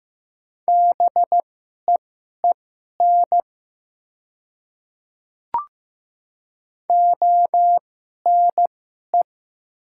Morse code, heard as a keyed tone, about 15 wpm, spaced as for 11 wpm, 700 Hz. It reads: BEEN ONE
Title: 2x Character Spacing - Top 100 Words - 15wpm